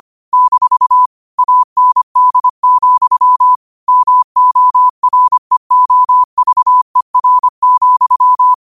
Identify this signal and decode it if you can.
Morse code, heard as a keyed tone, about 25 wpm, 1000 Hz. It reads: - AND, MOREOVER,
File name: qotdfiles / 25wpm